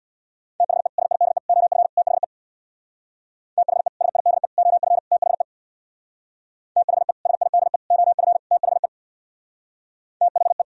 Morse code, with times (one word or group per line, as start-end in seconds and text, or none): 0.6-0.9 THE
1.0-1.4 SIDE
1.5-1.9 OF
2.0-2.2 THE
3.6-3.9 THE
4.0-4.5 SIDE
4.6-5.0 OF
5.1-5.4 THE
6.8-7.1 THE
7.3-7.8 SIDE
7.9-8.4 OF
8.5-8.9 THE
10.2-10.6 THE